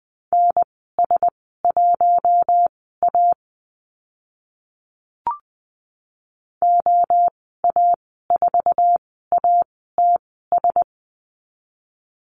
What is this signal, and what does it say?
Morse code, heard as a keyed tone, about 20 wpm, spaced as for 14 wpm, 700 Hz. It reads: NS1A OA4ATS
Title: 076 Lesson 13g - Letter L - Call Signs 20wpm